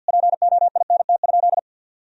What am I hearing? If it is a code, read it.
Morse code, heard as a keyed tone, about 48 wpm, 700 Hz. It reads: POINT?
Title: Sentences from Top 1000 Words - Review Part 1 - 50wpm